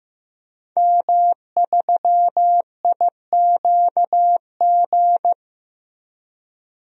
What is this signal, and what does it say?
Morse code, heard as a keyed tone, about 16 wpm, 700 Hz. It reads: M3IQG
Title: DXCC-100 Call Signs - 5 Characters - Repeated 2x - 15wpm